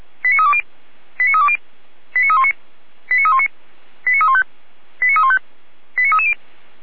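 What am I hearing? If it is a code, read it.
Five-tone selective call (ZVEI), heard as quick runs of five tones, tones of about 80 ms. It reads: 89320 89320 89319 89319 89315 89315 89330
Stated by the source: ZVEI five-tone call 89110 received with an SDR.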